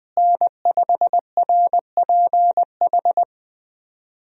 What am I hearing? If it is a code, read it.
Morse code, heard as a keyed tone, about 20 wpm, 700 Hz. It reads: N5RPH
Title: USA Call Signs - 5 Characters - Repeated 2x - 20wpm